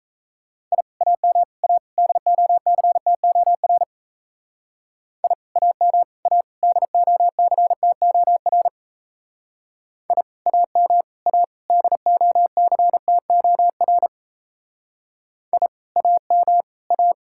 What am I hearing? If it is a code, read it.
Morse code, heard as a keyed tone, about 35 wpm, 700 Hz. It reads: I AM A DOCTOR I AM A DOCTOR I AM A DOCTOR I AM A